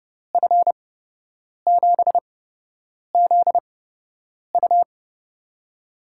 Morse code, heard as a keyed tone, about 29 wpm, spaced as for 8 wpm, 700 Hz. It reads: F7ZU
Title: USA Call Signs - 5 Characters - 8x Character Spacing - 30wpm